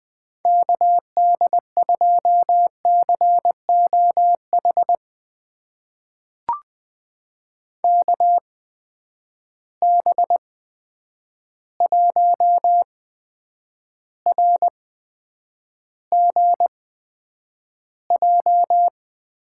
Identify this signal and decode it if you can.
Morse code, heard as a keyed tone, about 20 wpm, spaced as for 9 wpm, 700 Hz. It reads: KD2COH KB1RGJ
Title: USA Call Signs - 6 Characters - 8x Character Spacing - 20wpm